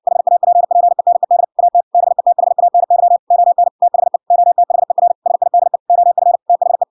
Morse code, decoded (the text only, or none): HAPPENED AT BETHANY ON THE OTHER SIDE OF THE